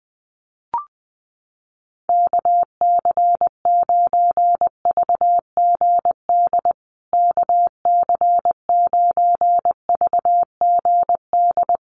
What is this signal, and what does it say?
Morse code, heard as a keyed tone, about 20 wpm, 700 Hz. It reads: KC9VGD KC9VGD
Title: USA Call Signs Repeated 3x 20wpm